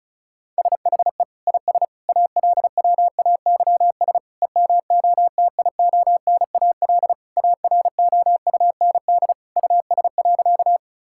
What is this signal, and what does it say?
Morse code, heard as a keyed tone, about 34 wpm, 700 Hz. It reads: SHE IS ALWAYS EMOTIONAL AROUND US.
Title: Sentences from Top 1000 Words - Review Part 5 - 35wpm